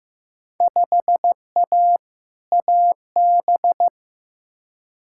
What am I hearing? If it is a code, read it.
Morse code, heard as a keyed tone, about 15 wpm, 700 Hz. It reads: 5A AB
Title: Field Day Exchanges Encore - 15wpm